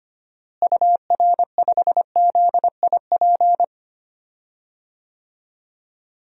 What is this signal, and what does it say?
Morse code, heard as a keyed tone, about 25 wpm, 700 Hz. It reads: UR5ZIP
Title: DXCC-100 Call Signs - 6 Characters - Repeated 3x - 25wpm